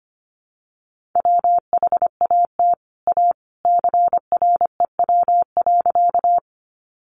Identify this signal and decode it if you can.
Morse code, heard as a keyed tone, about 25 wpm, 700 Hz. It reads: WHAT A CREW.
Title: Sentences from Top 1500 Words 25wpm